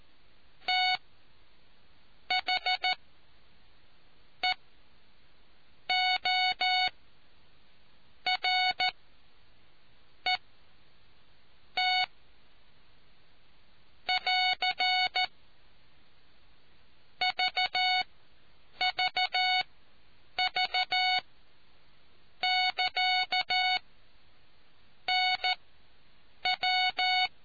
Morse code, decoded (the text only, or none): THEORET <AR> VVV<KA>NW